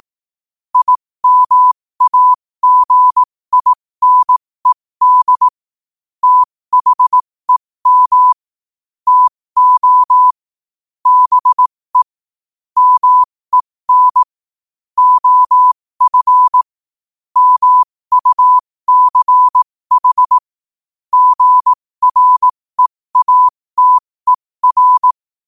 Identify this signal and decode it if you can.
Morse code, heard as a keyed tone, about 18 wpm, spaced as for 15 wpm, 1000 Hz. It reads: IMAGINED THEM TO BE MEN OF MUCH GREATER
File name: qotdfiles / 15wpm